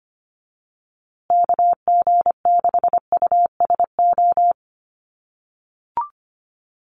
Morse code, heard as a keyed tone, about 25 wpm, 700 Hz. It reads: KG6USO
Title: USA Call Signs Repeated 3x 25wpm